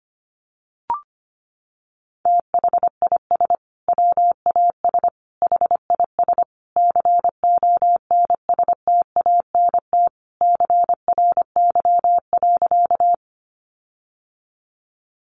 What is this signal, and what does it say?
Morse code, heard as a keyed tone, about 25 wpm, 700 Hz. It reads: THIS WAS HIS CONSTANT CRY.
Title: Sentences from Top 1500 Words 25wpm